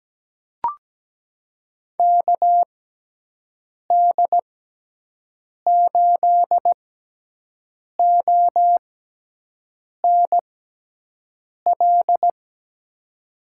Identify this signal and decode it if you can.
Morse code, heard as a keyed tone, about 17 wpm, spaced as for 6 wpm, 700 Hz. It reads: KD8ONL
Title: USA Call Signs - 6 Characters - 6x Character Spacing - 17wpm